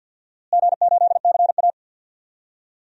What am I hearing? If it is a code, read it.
Morse code, heard as a keyed tone, about 51 wpm, 700 Hz. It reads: G8CA
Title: DXCC-100 Call Signs - 4 Characters - Repeated 2x - 50wpm